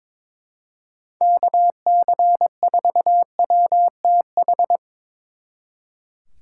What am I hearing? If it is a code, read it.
Morse code, heard as a keyed tone, about 25 wpm, 700 Hz. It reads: KC4WTH
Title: Speed Racing - USA 2x3 Call Signs 22wpm